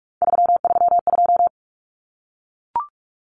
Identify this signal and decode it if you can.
Morse code, heard as a keyed tone, about 44 wpm, 700 Hz. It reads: 332
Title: Common Cut Numbers - 3-Digit 45wpm